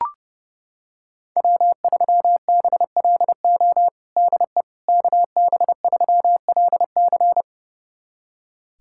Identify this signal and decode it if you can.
Morse code, heard as a keyed tone, about 30 wpm, 700 Hz. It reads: W3BLO DE KB3LC
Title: USA Callsign DE Callsign - 5 Characters - 30wpm